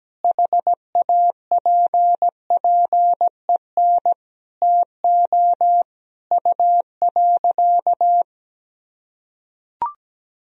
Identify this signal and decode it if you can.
Morse code, heard as a keyed tone, about 17 wpm, 700 Hz. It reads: HAPPEN TO U.
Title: Sentences from Top 500 Words and Abbreviations 17wpm